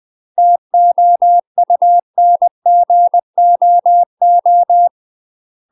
Morse code, heard as a keyed tone, about 20 wpm, 700 Hz. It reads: TOUNGOO
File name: UK MASTER QSOs TRAINING / 20WPM / UK MASTER QSOs 0049